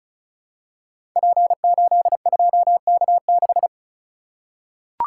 Morse code, heard as a keyed tone, about 36 wpm, 700 Hz. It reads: P82K6
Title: Code Groups - 5 Letter:Numbers - 35wpm